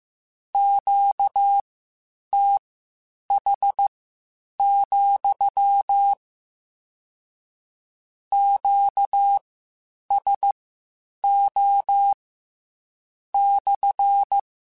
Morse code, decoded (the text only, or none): QTH, QSO /